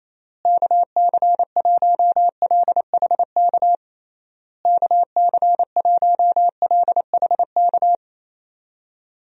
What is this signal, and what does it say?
Morse code, heard as a keyed tone, about 28 wpm, 700 Hz. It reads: KC1LHK KC1LHK
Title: USA Call Signs - 6 Characters - Repeated 3x - 28wpm